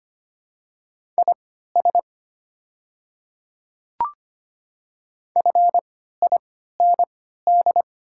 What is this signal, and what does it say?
Morse code, heard as a keyed tone, about 25 wpm, spaced as for 14 wpm, 700 Hz. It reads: IS FIND
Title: 3x Character Spacing - Top 100 Words - 25wpm